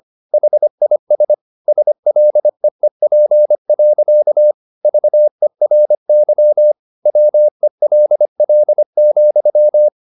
Morse code, heard as a keyed tone, about 25 wpm, 600 Hz. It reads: HIS SLEEP. VERY WELL,
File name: the old man and the sea part 1 25wpm